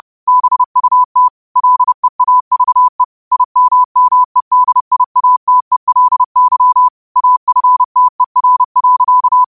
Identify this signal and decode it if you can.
Morse code, heard as a keyed tone, about 30 wpm, 1000 Hz. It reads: BUT LEAVE IMMEDIATELY AFTER.
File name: qotdfiles / 30wpm